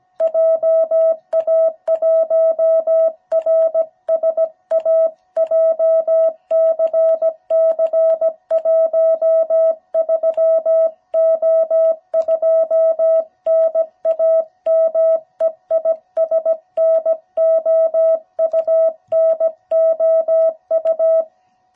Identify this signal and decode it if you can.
Morse code, heard as a keyed tone, about 17 wpm, 660 Hz. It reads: JA1RSA JCC13O2 NAME IS NOU NOU
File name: jh8cbh / 2021cw